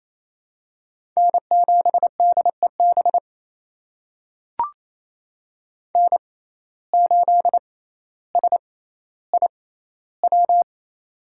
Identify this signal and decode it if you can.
Morse code, heard as a keyed tone, about 27 wpm, spaced as for 14 wpm, 700 Hz. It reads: N7DEB N8SIW
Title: USA Call Signs - 5 Characters - 6x Character Spacing - 28wpm